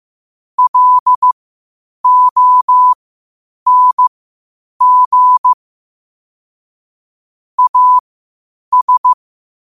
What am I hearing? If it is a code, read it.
Morse code, heard as a keyed tone, about 15 wpm, spaced as for 8 wpm, 1000 Hz. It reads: LONG AS
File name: qotdfiles / 8wpm